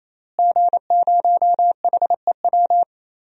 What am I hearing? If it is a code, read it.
Morse code, heard as a keyed tone, about 28 wpm, 690 Hz. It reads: G0HEW
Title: DXCC-100 Call Signs - 5 Characters - Repeated 3x - 28wpm